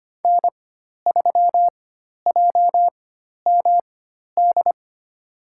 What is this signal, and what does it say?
Morse code, heard as a keyed tone, about 25 wpm, spaced as for 12 wpm, 700 Hz. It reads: N3JMD
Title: USA Call Signs - 4x Character Spacing - 25wpm